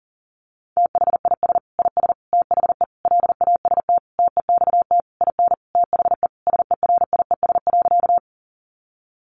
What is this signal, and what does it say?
Morse code, heard as a keyed tone, about 40 wpm, 700 Hz. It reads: THIS IS THE LAST TEXT IN THE SERIES.
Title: Sentences from Top 1200 Words 40wpm